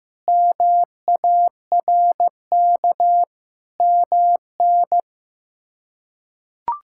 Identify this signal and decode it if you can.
Morse code, heard as a keyed tone, about 15 wpm, 700 Hz. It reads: MARK MN
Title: K1USN SST QSO Practice - 15wpm